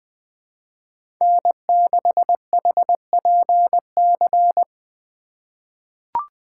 Morse code, heard as a keyed tone, about 20 wpm, 700 Hz. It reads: N6HPC
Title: USA Call Signs - 5 Characters - 3x Character Spacing - 20wpm